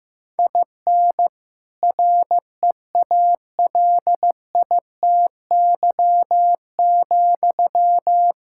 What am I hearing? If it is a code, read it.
Morse code, heard as a keyed tone, about 15 wpm, 700 Hz. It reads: IN REALITY,
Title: Sentences from Top 1600 Words 15wpm